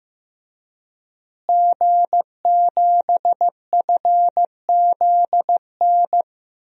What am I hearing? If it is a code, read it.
Morse code, heard as a keyed tone, about 15 wpm, 700 Hz. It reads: G7FZN
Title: DXCC-100 Call Signs - 5 Characters - Repeated 2x - 15wpm